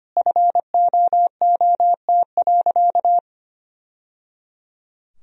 Morse code, heard as a keyed tone, about 25 wpm, 700 Hz. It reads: FOOT.
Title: Sentences from Top 1000 Words - Review Part 3 - 25wpm